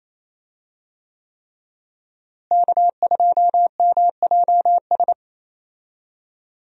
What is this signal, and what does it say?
Morse code, heard as a keyed tone, about 28 wpm, 700 Hz. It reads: K2MJS
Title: USA Call Signs - 5 Characters - Repeated 3x - 28wpm